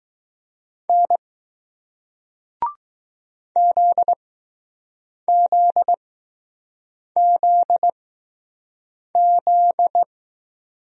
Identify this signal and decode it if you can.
Morse code, heard as a keyed tone, about 20 wpm, spaced as for 6 wpm, 700 Hz. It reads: N ZZZZ